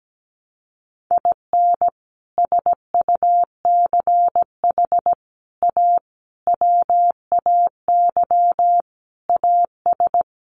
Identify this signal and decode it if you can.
Morse code, heard as a keyed tone, about 17 wpm, 700 Hz. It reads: IN SUCH A WAY AS
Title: Sets of 5 Words Encore 17wpm